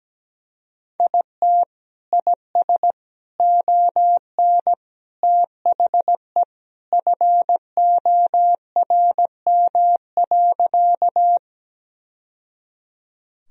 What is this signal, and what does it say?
Morse code, heard as a keyed tone, about 17 wpm, 700 Hz. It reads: IT IS ON THE FORM.
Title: Sentences from Top 1000 Words - Review Part 1 - 17wpm